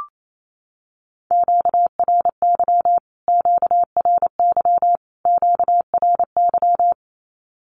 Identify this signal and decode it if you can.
Morse code, heard as a keyed tone, about 28 wpm, 700 Hz. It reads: QRY QRY QRY
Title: Q-Codes Repeated 3x 28wpm